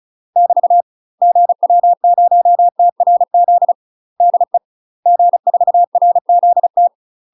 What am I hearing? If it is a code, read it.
Morse code, heard as a keyed tone, about 36 wpm, 700 Hz. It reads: <BT> GW0TRZ DE G4RZT
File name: UK MASTER QSOs TRAINING / 35WPM / UK MASTER QSOs 0041